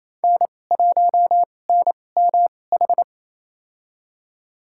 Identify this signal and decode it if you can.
Morse code, heard as a keyed tone, about 28 wpm, spaced as for 20 wpm, 700 Hz. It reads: N1NMH
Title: USA Call Signs - 5 Characters - 2x Character Spacing - 28wpm